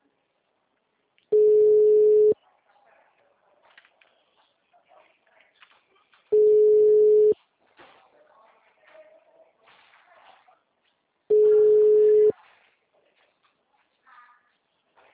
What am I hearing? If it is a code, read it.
Ringback tone (Europe, 425 Hz).